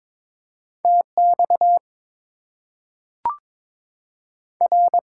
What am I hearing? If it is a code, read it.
Morse code, heard as a keyed tone, about 22 wpm, 700 Hz. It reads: TX R
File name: Top 90 QSO Elements - 8x Character Spacing - 22wpm